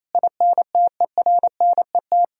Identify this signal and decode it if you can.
Morse code, heard as a keyed tone, about 28 wpm, 700 Hz. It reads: INTERNET
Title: Sentences from Top 1000 Words - Review Part 4 - 28wpm